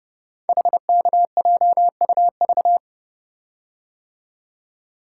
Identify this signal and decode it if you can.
Morse code, heard as a keyed tone, about 29 wpm, 700 Hz. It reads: HKJUV
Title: Code Groups - 5 Letters - 30wpm